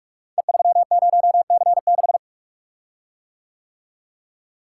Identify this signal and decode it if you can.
Morse code, heard as a keyed tone, about 44 wpm, 690 Hz. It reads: E30C6
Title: Code Groups - 5 Letter:Numbers - 45wpm